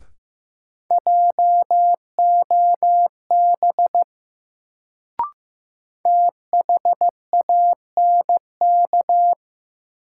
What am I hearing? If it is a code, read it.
Morse code, heard as a keyed tone, about 15 wpm, 700 Hz. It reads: JOB THANK
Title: Sentences from Top 600 Words - New Word Drill - 15wpm